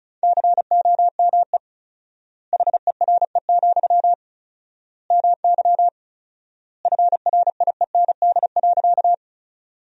Morse code, heard as a keyed tone, about 34 wpm, 700 Hz. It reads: COME HERE, MY FRIEND.
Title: 4x Word-Spacing - Sentences from Top 300 Words - 35wpm